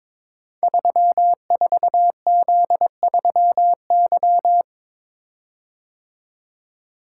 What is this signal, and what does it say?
Morse code, heard as a keyed tone, about 23 wpm, 700 Hz. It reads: 34Z3Y
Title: Code Groups - 5 Letter:Numbers - 22wpm